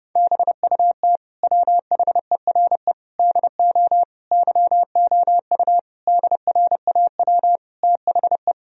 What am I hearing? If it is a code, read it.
Morse code, heard as a keyed tone, about 32 wpm, 700 Hz. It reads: BUT WHERE DO YOU DRAW THE
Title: Sentences from Top 1000 Words - Review Part 1 - 30wpm